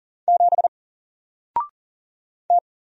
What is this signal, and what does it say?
Morse code, heard as a keyed tone, about 39 wpm, 700 Hz. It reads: 7 T
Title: Single Letter-Number Alt 40wpm